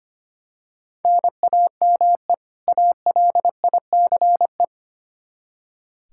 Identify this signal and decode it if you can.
Morse code, heard as a keyed tone, about 25 wpm, 700 Hz. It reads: NAME ALICE
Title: Practice QSO - Exchange 1 - Beginning - 25wpm